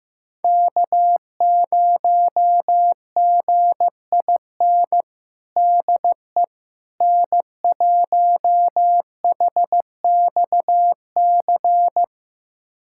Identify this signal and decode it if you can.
Morse code, heard as a keyed tone, about 15 wpm, 700 Hz. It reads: K0GIN DE N1HXC
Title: USA Callsign DE Callsign - 5 Characters - 15wpm